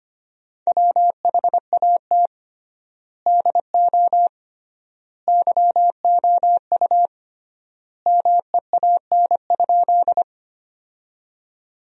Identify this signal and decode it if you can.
Morse code, heard as a keyed tone, about 25 wpm, 700 Hz. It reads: WHAT DO YOU MEAN?
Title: 3x Word-Spacing - Sentences from Top 200 Words - 25wpm